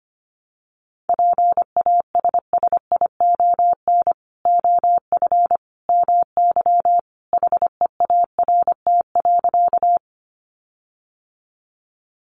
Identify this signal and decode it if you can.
Morse code, heard as a keyed tone, about 25 wpm, 700 Hz. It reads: PASSION OF MY HEART.
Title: Sentences from Top 1300 Words 25wpm